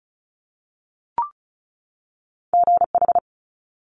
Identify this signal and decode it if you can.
Morse code, heard as a keyed tone, about 36 wpm, 700 Hz. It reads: GH
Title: Multi-letter Phonemes 35wpm